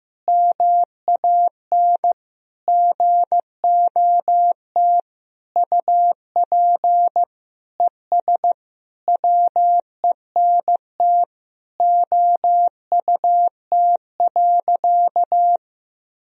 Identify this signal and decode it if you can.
Morse code, heard as a keyed tone, about 16 wpm, 700 Hz. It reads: MAN GOT UP ES WENT OUT.
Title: Sentences from Top 300 Words and Abbreviations 15wpm